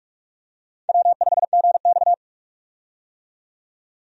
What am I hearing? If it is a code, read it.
Morse code, heard as a keyed tone, about 46 wpm, 700 Hz. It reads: W5GX
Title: USA Call Signs - 4 Characters - 45wpm